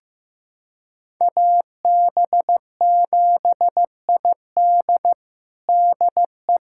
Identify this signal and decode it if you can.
Morse code, heard as a keyed tone, about 15 wpm, 700 Hz. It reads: AB7ID DE
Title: USA Callsign DE Callsign 15wpm